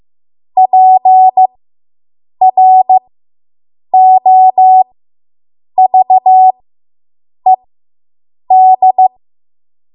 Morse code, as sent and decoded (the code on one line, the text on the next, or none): .--. .-. --- ...- . -..
PROVED